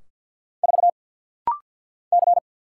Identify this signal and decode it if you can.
Morse code, heard as a keyed tone, about 50 wpm, 700 Hz. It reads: 4 C
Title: Single Letter-Number Alt - Rapid-Fire 50wpm